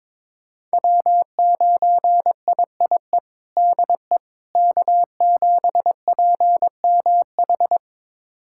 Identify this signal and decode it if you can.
Morse code, heard as a keyed tone, about 21 wpm, 700 Hz. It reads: W9IIE DE K7PMH
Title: USA Callsign DE Callsign - 5 Characters - 22wpm